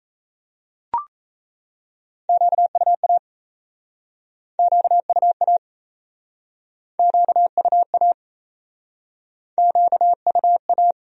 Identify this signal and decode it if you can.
Morse code, heard as a keyed tone, about 37 wpm, 700 Hz. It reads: QUA QUA QUA QUA